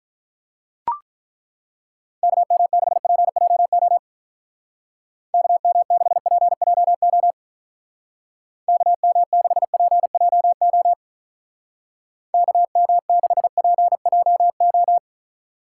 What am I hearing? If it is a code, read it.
Morse code, heard as a keyed tone, about 42 wpm, 700 Hz. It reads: KM6PJO KM6PJO KM6PJO KM6PJO
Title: Speed Racing - USA Call Signs 35wpm